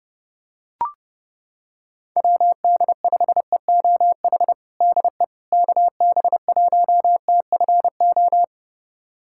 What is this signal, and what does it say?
Morse code, heard as a keyed tone, about 32 wpm, 700 Hz. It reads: WD5EOH DE KB1TFO
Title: USA Callsign DE Callsign - 6 Characters - 30wpm